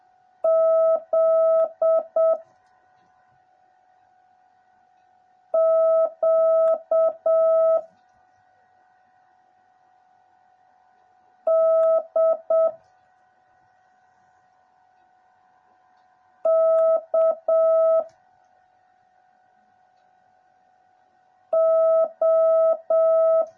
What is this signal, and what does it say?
Morse code, heard as a keyed tone, about 7 wpm, 650 Hz. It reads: Z Q D K O
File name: jh8cbh / 2021cw / cw09a